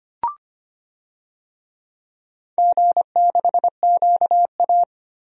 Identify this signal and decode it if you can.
Morse code, heard as a keyed tone, about 25 wpm, 700 Hz. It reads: G6QA
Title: DXCC-100 Call Signs - 4 Characters - Repeated 2x - 25wpm